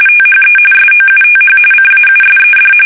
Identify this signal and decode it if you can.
RTTY, about 45.45 baud. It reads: THE SOUTH CAUSED
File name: fsk850 / usnavy3